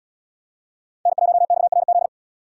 Morse code, heard as a keyed tone, about 69 wpm, 700 Hz. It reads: N1XRZ